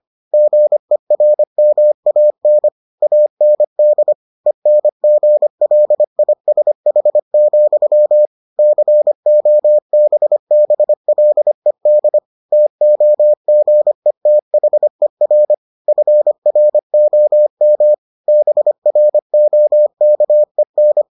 Morse code, heard as a keyed tone, about 25 wpm, 600 Hz. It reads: GERMAN AND ENGLISH, COBBLED TOGETHER FROM BROKEN